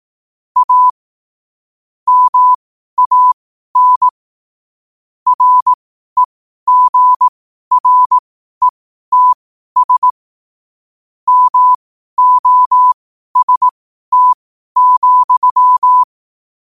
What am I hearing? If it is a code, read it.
Morse code, heard as a keyed tone, about 18 wpm, spaced as for 12 wpm, 1000 Hz. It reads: A MAN REGRETS MOST,
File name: qotdfiles / 12wpm